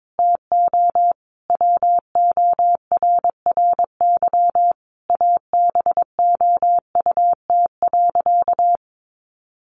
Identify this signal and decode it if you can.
Morse code, heard as a keyed tone, about 22 wpm, 700 Hz. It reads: TO WORRY ABOUT.
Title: Sentences from Top 1100 Words 22wpm